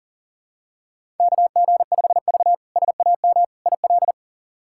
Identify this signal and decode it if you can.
Morse code, heard as a keyed tone, about 40 wpm, 700 Hz. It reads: KG5V SAM IL
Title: K1USN SST QSO Practice - 40wpm